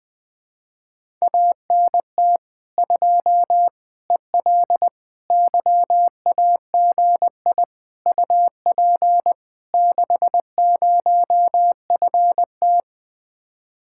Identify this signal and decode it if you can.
Morse code, heard as a keyed tone, about 20 wpm, 700 Hz. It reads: ANT 2 EL YAGI UP 60FT
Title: Speed-Racing Common QSO Phrases Encore - 17wpm